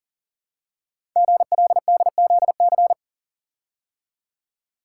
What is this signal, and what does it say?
Morse code, heard as a keyed tone, about 43 wpm, 700 Hz. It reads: GLDZC
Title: Code Groups - 5 Letters - 40wpm